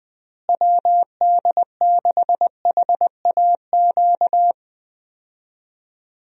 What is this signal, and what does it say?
Morse code, heard as a keyed tone, about 21 wpm, 700 Hz. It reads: WD6HAQ
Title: USA Call Signs - 6 Characters - Repeated 2x - 20wpm